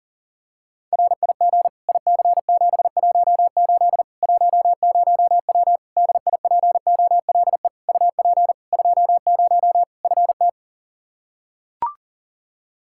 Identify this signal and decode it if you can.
Morse code, heard as a keyed tone, about 40 wpm, 700 Hz. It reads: RIG IC718 10W DIPOLE UP 20 FT
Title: Practice QSO - Exchange 2 - Info - 40wpm